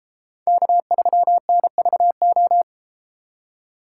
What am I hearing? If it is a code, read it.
Morse code, heard as a keyed tone, about 31 wpm, 700 Hz. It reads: K3NVO